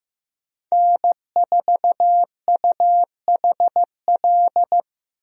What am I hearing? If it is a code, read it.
Morse code, heard as a keyed tone, about 15 wpm, 700 Hz. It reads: N4UHL
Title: USA Call Signs - 5 Characters - Repeated 3x - 15wpm